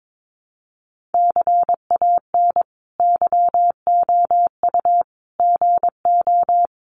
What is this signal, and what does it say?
Morse code, heard as a keyed tone, about 22 wpm, 700 Hz. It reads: CAN YOU GO
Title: Sentences from Top 1500 Words 22wpm